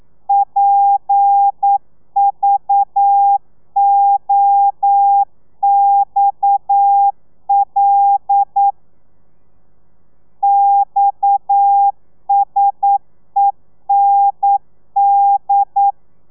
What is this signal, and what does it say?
Morse code, heard as a keyed tone, about 9 wpm, 800 Hz. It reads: PVOXL XSEND
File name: telegrafia / i2tfj / Lez7 40